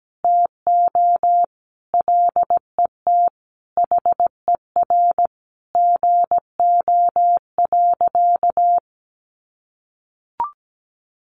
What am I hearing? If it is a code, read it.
Morse code, heard as a keyed tone, about 17 wpm, 700 Hz. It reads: TO LET HER GO.
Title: Sentences from Top 1500 Words 17wpm